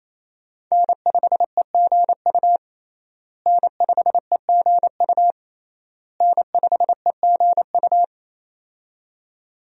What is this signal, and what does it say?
Morse code, heard as a keyed tone, about 28 wpm, 700 Hz. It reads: N5EGU N5EGU N5EGU
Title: USA Call Signs - 5 Characters - Repeated 3x - 28wpm